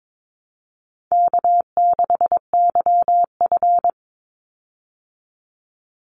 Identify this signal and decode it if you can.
Morse code, heard as a keyed tone, about 22 wpm, 700 Hz. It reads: K6YF
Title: Easy Call Signs Repeated 3x - 22wpm